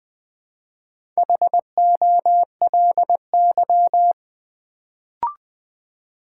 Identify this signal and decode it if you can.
Morse code, heard as a keyed tone, about 20 wpm, 700 Hz. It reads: HOLY
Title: Build A Word - 1 Letter At A Time - 3 to 5-Letter Words - 20wpm